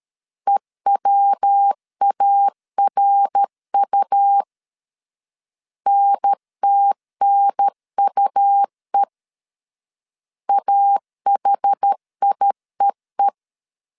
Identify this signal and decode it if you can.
Morse code, heard as a keyed tone, about 13 wpm, 800 Hz. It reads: EWARU NTNUE AHIEE
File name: CW / DET / 1 3 1500C S C 12wpm 2750